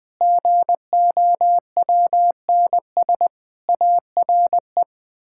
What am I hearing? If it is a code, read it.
Morse code, heard as a keyed tone, about 21 wpm, 700 Hz. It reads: GOWNS ARE
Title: Pangram Challenge v2 - 20wpm